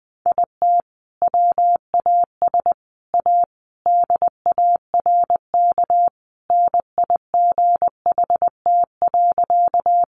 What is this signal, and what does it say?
Morse code, heard as a keyed tone, about 20 wpm, 700 Hz. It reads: IT WAS A DARK NIGHT.
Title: Sentences from Top 500 Words 20wpm